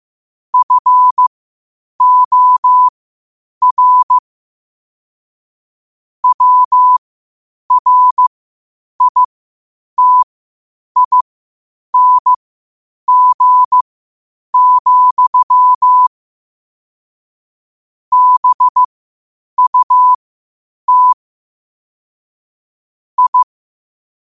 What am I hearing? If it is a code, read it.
Morse code, heard as a keyed tone, about 15 wpm, spaced as for 8 wpm, 1000 Hz. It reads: FOR WRITING, BUT I
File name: qotdfiles / 8wpm